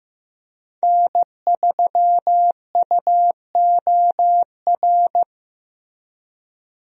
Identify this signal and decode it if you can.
Morse code, heard as a keyed tone, about 15 wpm, 700 Hz. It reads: N3UOR
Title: USA Call Signs - 5 Characters - Repeated 2x - 15wpm